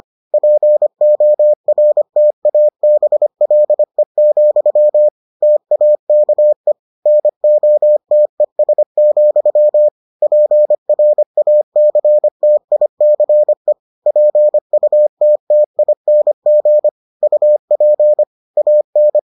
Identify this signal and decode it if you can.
Morse code, heard as a keyed tone, about 25 wpm, 600 Hz. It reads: PORTABLE, TAKE NOTES, PRACTICE PUTTING UP AN